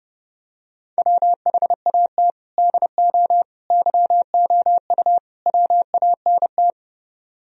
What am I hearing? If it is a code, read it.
Morse code, heard as a keyed tone, about 28 wpm, 700 Hz. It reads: WHAT DO YOU WANT